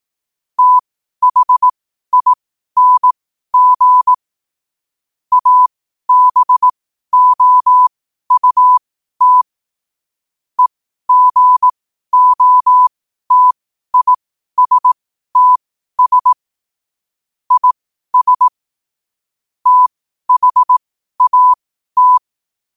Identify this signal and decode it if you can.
Morse code, heard as a keyed tone, about 18 wpm, spaced as for 12 wpm, 1000 Hz. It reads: THING ABOUT EGOTISTS IS THAT
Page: Quotes for Wed, 13 Aug 2025 in Morse Code at 12 words per minute.